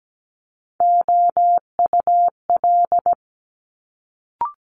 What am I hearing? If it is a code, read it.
Morse code, heard as a keyed tone, about 17 wpm, 700 Hz. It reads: OUL
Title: Multi-letter Phonemes Quick Fire 17wpm